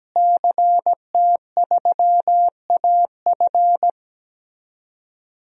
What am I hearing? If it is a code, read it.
Morse code, heard as a keyed tone, about 17 wpm, 700 Hz. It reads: CT3AF
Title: DXCC-100 Call Signs - 5 Characters - Repeated 2x - 17wpm